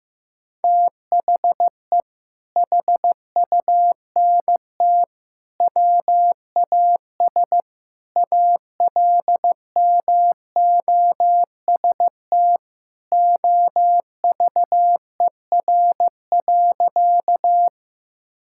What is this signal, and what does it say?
Morse code, heard as a keyed tone, about 15 wpm, 700 Hz. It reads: THE HUNT WAS ALMOST OVER.
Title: Sentences from Top 1600 Words 15wpm